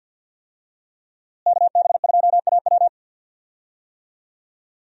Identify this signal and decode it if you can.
Morse code, heard as a keyed tone, about 52 wpm, 700 Hz. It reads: KB2AW
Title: USA Call Signs - 5 Characters - Repeated 3x - 50wpm